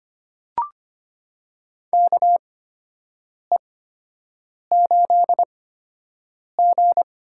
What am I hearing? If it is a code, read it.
Morse code, heard as a keyed tone, about 25 wpm, spaced as for 7 wpm, 700 Hz. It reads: KE8G
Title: USA Call Signs - 8x Character Spacing - 25wpm